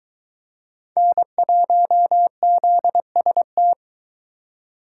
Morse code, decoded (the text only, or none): N1ZST